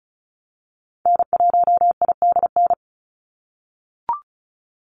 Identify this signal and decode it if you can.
Morse code, heard as a keyed tone, about 36 wpm, 700 Hz. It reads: N1IDN
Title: NA SOTA Activator Callsigns 35wpm